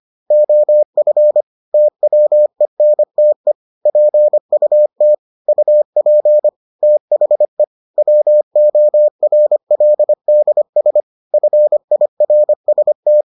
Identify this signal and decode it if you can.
Morse code, heard as a keyed tone, about 25 wpm, 600 Hz. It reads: OF TWENTE PUT UP THE WORLDS FIRST